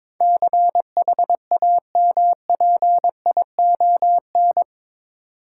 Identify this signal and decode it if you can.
Morse code, heard as a keyed tone, about 22 wpm, 700 Hz. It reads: CHAMPION
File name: practice / 22wpm / Eight Letter Words - 22wpm